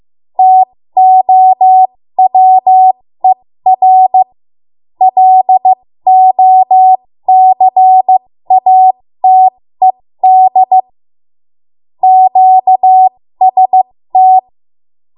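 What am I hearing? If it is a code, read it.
Morse code, heard as a keyed tone, about 15 wpm, spaced as for 13 wpm, 750 Hz. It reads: TOWER LOCATED QST
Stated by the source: 13 wpm Morse code practice archive transmission.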